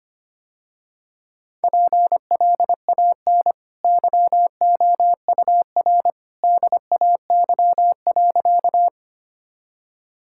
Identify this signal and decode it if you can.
Morse code, heard as a keyed tone, about 25 wpm, 700 Hz. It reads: PLAN YOUR DAY.
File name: Easy Sentences with Hints - 25wpm